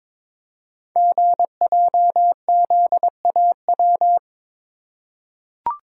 Morse code, heard as a keyed tone, about 22 wpm, 700 Hz. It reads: GJZAW
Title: Code Groups - 5 Letters - 22wpm